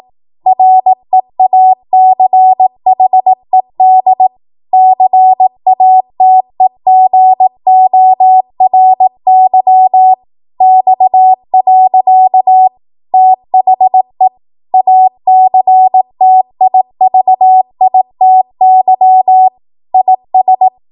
Morse code, heard as a keyed tone, about 18 wpm, 750 Hz. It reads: REACHED CATEGORY X. THE ACTIVITY IS
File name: Morse / 18WPM